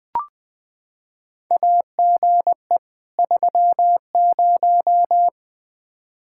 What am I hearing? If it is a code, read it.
Morse code, heard as a keyed tone, about 20 wpm, 700 Hz. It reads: AGE 30
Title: Practice QSO - Exchange 3 - More Info - 20wpm